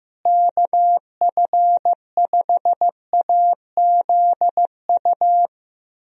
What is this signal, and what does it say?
Morse code, heard as a keyed tone, about 16 wpm, 700 Hz. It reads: KF5AZU
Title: USA Call Signs - 6 Characters - Repeated 2x - 15wpm